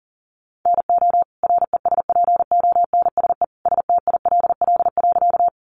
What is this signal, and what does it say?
Morse code, heard as a keyed tone, about 40 wpm, 700 Hz. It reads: NO RESPONSE STILL.
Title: Sentences from Top 1400 Words 40wpm